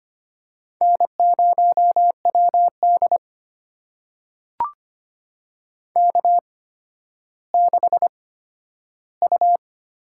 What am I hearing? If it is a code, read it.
Morse code, heard as a keyed tone, about 25 wpm, spaced as for 12 wpm, 700 Hz. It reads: N0WD K6U
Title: USA Call Signs - 4 Characters - 8x Character Spacing - 25wpm